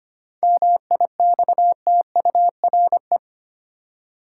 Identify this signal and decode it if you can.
Morse code, heard as a keyed tone, about 25 wpm, 700 Hz. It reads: MIXTURE
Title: Seven Letter Words - 25wpm